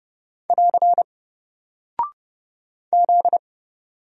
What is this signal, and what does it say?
Morse code, heard as a keyed tone, about 30 wpm, 700 Hz. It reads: <AR> Z
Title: Single Character (Letter:Number:Punctuation:Prosign) Alt 30wpm